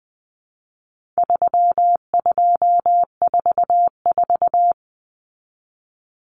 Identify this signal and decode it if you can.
Morse code, heard as a keyed tone, about 20 wpm, 700 Hz. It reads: 3244
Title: Common Cut Numbers - 4-Digit 20wpm